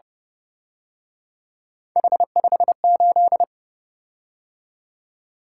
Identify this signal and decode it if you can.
Morse code, heard as a keyed tone, about 31 wpm, 700 Hz. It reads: H58
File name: Speed-Racing - Three Letter-Number Combinations - 25wpm